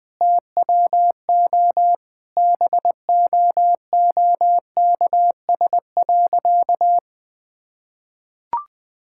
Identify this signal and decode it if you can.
Morse code, heard as a keyed tone, about 20 wpm, 700 Hz. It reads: TWO BOOKS.
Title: Easy Sentences with Hints - 20wpm